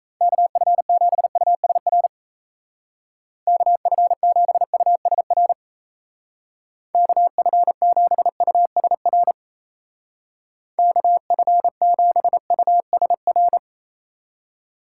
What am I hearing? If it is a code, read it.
Morse code, heard as a keyed tone, about 33 wpm, 700 Hz. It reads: KF7USR KF7USR KF7USR KF7USR
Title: Speed Racing - USA 2x3 Call Signs 28wpm